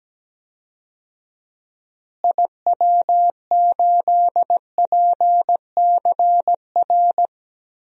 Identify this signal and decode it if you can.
Morse code, heard as a keyed tone, about 17 wpm, 700 Hz. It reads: IW8PCR
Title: DXCC-100 Call Signs - 6 Characters - Repeated 3x - 17wpm